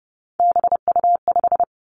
Morse code, heard as a keyed tone, about 30 wpm, 700 Hz. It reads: BU5
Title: Three Challenging Letter-Number Combinations 30wpm